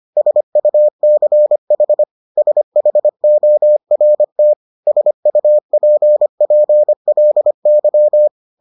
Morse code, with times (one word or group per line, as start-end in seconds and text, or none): 0.2-2.0 SUCH
2.4-4.5 SHORT
4.9-8.3 SUPPLY